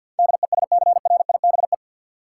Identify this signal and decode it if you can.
Morse code, heard as a keyed tone, about 52 wpm, 700 Hz. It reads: DESCRIBE
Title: Sentences from Top 1000 Words - New Word Drill - 50wpm